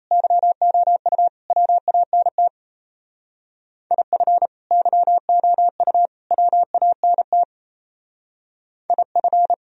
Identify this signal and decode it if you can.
Morse code, heard as a keyed tone, about 35 wpm, 700 Hz. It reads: YOU WANT IF YOU WANT IF